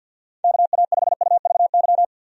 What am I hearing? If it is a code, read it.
Morse code, heard as a keyed tone, about 51 wpm, 700 Hz. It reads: KA5UVY
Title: USA Call Signs - 6 Characters - Repeated 2x - 50wpm